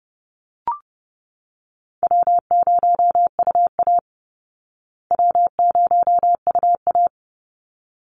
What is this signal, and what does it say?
Morse code, heard as a keyed tone, about 30 wpm, 700 Hz. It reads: W0UA W0UA
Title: Easy Call Signs Repeated 2x - 30wpm